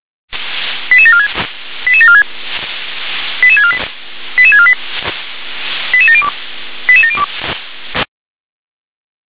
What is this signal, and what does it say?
Five-tone selective call (ZVEI), heard as quick runs of five tones, tones of about 80 ms. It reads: 88746 88746 88648 88648 88702 88702